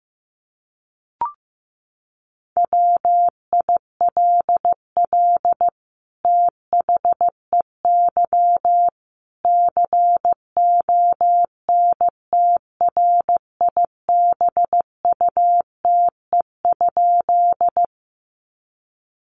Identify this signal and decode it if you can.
Morse code, heard as a keyed tone, about 15 wpm, 700 Hz. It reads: WILL THEY CONTRIBUTE?
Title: Sentences from Top 1400 Words 15wpm